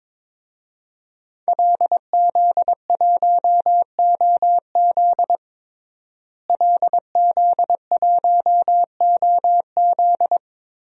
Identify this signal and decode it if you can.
Morse code, heard as a keyed tone, about 22 wpm, 700 Hz. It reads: LZ1OZ LZ1OZ
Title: DXCC-100 Call Signs - 5 Characters - Repeated 2x - 22wpm